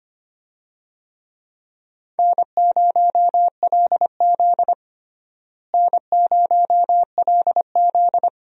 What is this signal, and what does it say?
Morse code, heard as a keyed tone, about 25 wpm, 700 Hz. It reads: N0LZ N0LZ
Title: USA Call Signs - 4 Characters - Repeated 2x - 25wpm